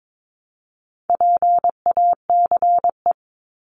Morse code, heard as a keyed tone, about 22 wpm, 700 Hz. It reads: PACE
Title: Top 1751-2000 Words 22wpm